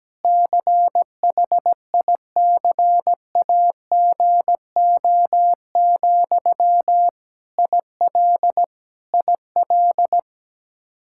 Morse code, heard as a keyed tone, about 17 wpm, 700 Hz. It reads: CHICAGO, IL IL
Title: K1USN SST QSO Practice 17wpm